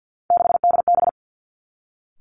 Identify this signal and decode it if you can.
Morse code, heard as a keyed tone, about 50 wpm, 700 Hz. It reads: 6DB
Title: Three Challenging Letter-Number Combinations 50wpm